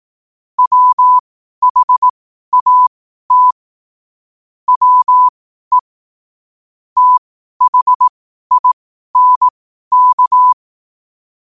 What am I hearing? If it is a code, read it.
Morse code, heard as a keyed tone, about 18 wpm, spaced as for 12 wpm, 1000 Hz. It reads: WHAT WE THINK
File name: 12wpm